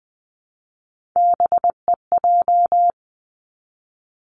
Morse code, heard as a keyed tone, about 20 wpm, 700 Hz. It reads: BEJ
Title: Three Challenging Letter-Number Combinations 20wpm